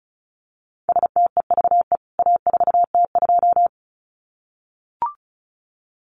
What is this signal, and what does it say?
Morse code, heard as a keyed tone, about 34 wpm, 700 Hz. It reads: STEVE A4T2
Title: CWT Exchange 35wpm